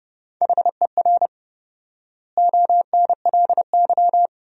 Morse code, heard as a keyed tone, about 32 wpm, 700 Hz. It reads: HER ONLY
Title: 4x Word-Spacing - Sentences from Top 400 Words - 30wpm